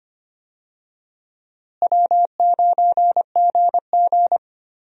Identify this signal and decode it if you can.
Morse code, heard as a keyed tone, about 25 wpm, 700 Hz. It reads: W9GG
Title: USA Call Signs - 4 Characters - Repeated 3x - 25wpm